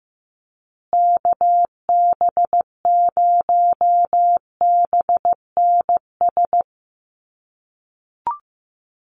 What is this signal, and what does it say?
Morse code, heard as a keyed tone, about 15 wpm, 700 Hz. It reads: KB0BNS
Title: USA Call Signs Repeated 2x 15wpm